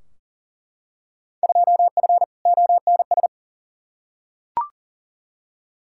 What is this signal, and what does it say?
Morse code, heard as a keyed tone, about 40 wpm, 700 Hz. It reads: 2F ONS
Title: Field Day Exchanges - 40wpm